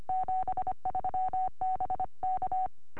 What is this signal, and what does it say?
Morse code, heard as a keyed tone, about 25 wpm, 750 Hz. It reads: 73 BK
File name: au / sm7gvf 2m